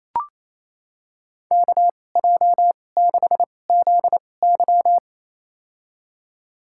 Morse code, heard as a keyed tone, about 28 wpm, spaced as for 20 wpm, 700 Hz. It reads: KJ6ZY
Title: USA Call Signs - 5 Characters - 2x Character Spacing - 28wpm